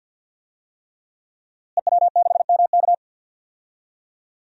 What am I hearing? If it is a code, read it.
Morse code, heard as a keyed tone, about 51 wpm, 700 Hz. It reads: EW6MK
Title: DXCC-100 Call Signs - 5 Characters - Repeated 3x - 50wpm